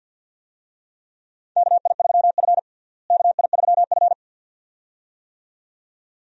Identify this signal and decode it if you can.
Morse code, heard as a keyed tone, about 52 wpm, 700 Hz. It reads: KI3F KI3F
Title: USA Call Signs - 4 Characters - Repeated 2x - 50wpm